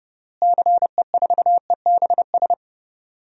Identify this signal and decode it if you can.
Morse code, heard as a keyed tone, about 32 wpm, 700 Hz. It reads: CE4EBS
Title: DXCC-100 Call Signs - 6 Characters - Repeated 2x - 30wpm